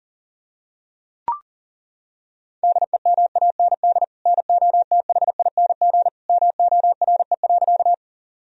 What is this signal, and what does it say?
Morse code, heard as a keyed tone, about 37 wpm, 700 Hz. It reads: DEMAND NOTHING MORE.
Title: Sentences from Top 1000 Words - Review Part 4 - 40wpm